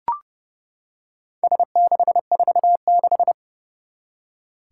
Morse code, heard as a keyed tone, about 29 wpm, 700 Hz. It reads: S646
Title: Four Challenging Letter-Number Combinations 30wpm